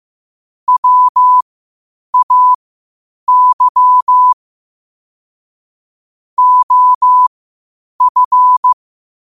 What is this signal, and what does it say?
Morse code, heard as a keyed tone, about 15 wpm, spaced as for 8 wpm, 1000 Hz. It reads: WAY OF